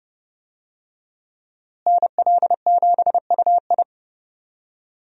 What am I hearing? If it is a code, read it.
Morse code, heard as a keyed tone, about 32 wpm, 700 Hz. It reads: NL7UI
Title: USA Call Signs - 5 Characters - Repeated 2x - 30wpm